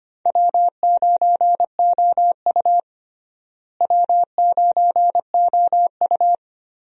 Morse code, heard as a keyed tone, about 25 wpm, 700 Hz. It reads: W9OU W9OU
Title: USA Call Signs - 4 Characters - Repeated 2x - 25wpm